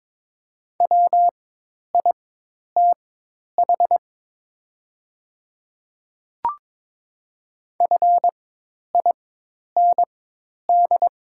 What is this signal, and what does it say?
Morse code, heard as a keyed tone, about 22 wpm, spaced as for 10 wpm, 700 Hz. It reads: WITH FIND
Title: 4x Character Spacing - Top 100 Words - 22wpm